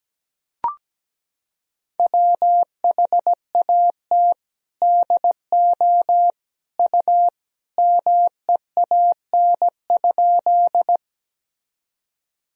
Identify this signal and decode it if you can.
Morse code, heard as a keyed tone, about 17 wpm, 700 Hz. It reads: WHAT DO U MEAN?
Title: Sentences from Top 200 Words and Abbreviations 17wpm